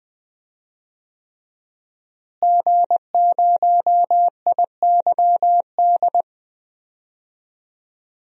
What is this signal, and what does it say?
Morse code, heard as a keyed tone, about 21 wpm, 700 Hz. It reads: G0IYD
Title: DXCC-100 Call Signs - 5 Characters - Repeated 3x - 20wpm